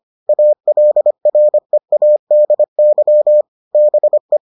Morse code, heard as a keyed tone, about 25 wpm, 600 Hz. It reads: ALREADY BE